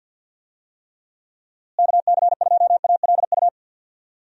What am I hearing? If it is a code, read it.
Morse code, heard as a keyed tone, about 52 wpm, 700 Hz. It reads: KC2ALU
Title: USA Call Signs - 6 Characters - Repeated 2x - 50wpm